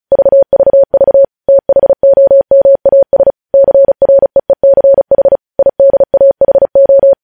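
Morse code, heard as a keyed tone, about 36 wpm, 550 Hz. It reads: VVV THOMAS CREECH IDAHO
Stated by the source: News headlines sent in Morse code at 35 WPM.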